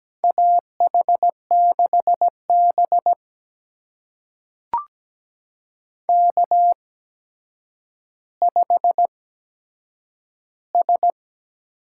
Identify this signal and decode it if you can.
Morse code, heard as a keyed tone, about 17 wpm, spaced as for 8 wpm, 700 Hz. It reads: AH6B K5S
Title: USA Call Signs - 4 Characters - 8x Character Spacing - 17wpm